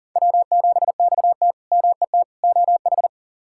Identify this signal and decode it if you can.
Morse code, heard as a keyed tone, about 40 wpm, 700 Hz. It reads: W7XT MET OH
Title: K1USN SST QSO Practice - 40wpm